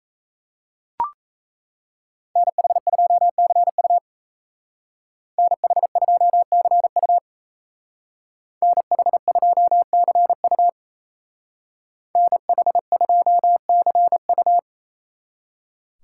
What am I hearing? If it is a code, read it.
Morse code, heard as a keyed tone, about 33 wpm, 700 Hz. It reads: NH2CU NH2CU NH2CU NH2CU
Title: Speed Racing - USA Call Signs 28wpm